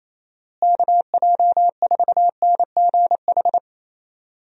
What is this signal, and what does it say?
Morse code, heard as a keyed tone, about 28 wpm, 700 Hz. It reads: KJ4NGH
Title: USA Call Signs - 6 Characters - Repeated 2x - 28wpm